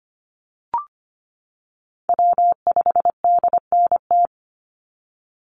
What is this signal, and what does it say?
Morse code, heard as a keyed tone, about 25 wpm, 700 Hz. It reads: W5DNT
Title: Call Signs - 25wpm